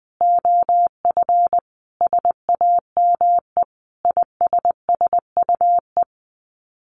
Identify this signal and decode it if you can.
Morse code, heard as a keyed tone, about 20 wpm, 700 Hz. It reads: OF SAME ISSUE
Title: Sets of 5 Words Encore2 20wpm